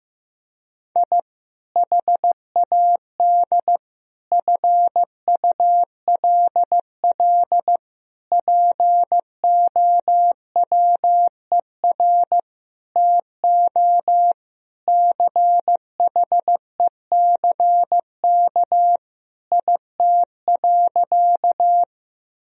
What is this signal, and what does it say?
Morse code, heard as a keyed tone, about 15 wpm, 700 Hz. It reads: I HAD FULL POWER TO CHECK IT.
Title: Sentences from Top 1000 Words - Review Part 3 - 15wpm